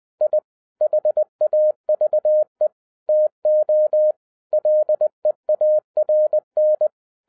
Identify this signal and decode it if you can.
Morse code, heard as a keyed tone, about 20 wpm, 610 Hz. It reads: I HAVE TO LEARN
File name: 2011-07-mp3 / n08 A LOT HAS CHANGED 20wpm